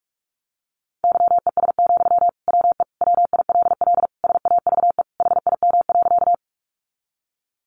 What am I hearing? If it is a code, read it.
Morse code, heard as a keyed tone, about 44 wpm, 700 Hz. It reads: YES, WE WILL SAVE HIM.
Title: Sentences from Top 1100 Words 45wpm